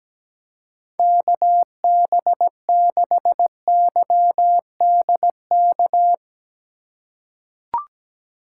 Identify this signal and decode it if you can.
Morse code, heard as a keyed tone, about 17 wpm, 700 Hz. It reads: KB6YDK
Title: USA Call Signs - 6 Characters - 6x Character Spacing - 17wpm